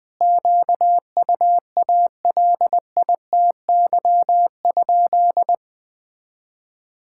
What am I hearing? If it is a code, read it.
Morse code, heard as a keyed tone, about 21 wpm, 700 Hz. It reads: QUALITY?
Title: Sentences from Top 1000 Words - Review Part 6 - 20wpm